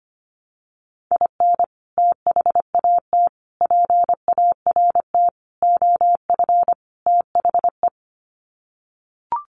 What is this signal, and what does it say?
Morse code, heard as a keyed tone, about 25 wpm, 700 Hz. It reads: IN THAT PART OF THE
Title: Sets of 5 Words Encore2 25wpm